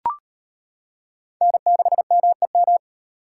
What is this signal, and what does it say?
Morse code, heard as a keyed tone, about 38 wpm, 700 Hz. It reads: N6MEM